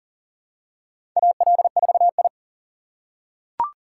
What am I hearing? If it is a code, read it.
Morse code, heard as a keyed tone, about 40 wpm, 700 Hz. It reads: AL4I
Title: USA Call Signs - 4 Characters - 2x Character Spacing - 40wpm